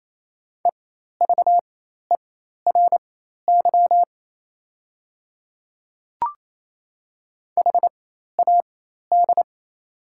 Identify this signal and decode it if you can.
Morse code, heard as a keyed tone, about 28 wpm, spaced as for 13 wpm, 700 Hz. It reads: EVERY HAD
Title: 4x Character Spacing - Top 200 Words - 28wpm